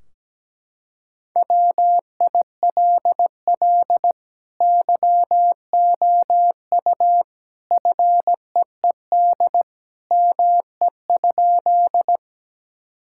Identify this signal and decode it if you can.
Morse code, heard as a keyed tone, about 17 wpm, 700 Hz. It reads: WILL YOU FEED ME?
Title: Sentences from Top 1600 Words 17wpm